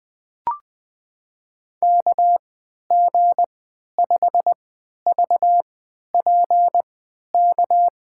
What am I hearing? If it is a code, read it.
Morse code, heard as a keyed tone, about 21 wpm, spaced as for 11 wpm, 700 Hz. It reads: KG5VPK
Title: USA Call Signs - 6 Characters - 3x Character Spacing - 20wpm